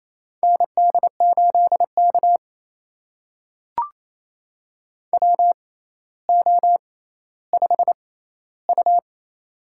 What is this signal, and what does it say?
Morse code, heard as a keyed tone, about 28 wpm, spaced as for 14 wpm, 700 Hz. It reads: ND8K WO5U
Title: USA Call Signs - 4 Characters - 6x Character Spacing - 28wpm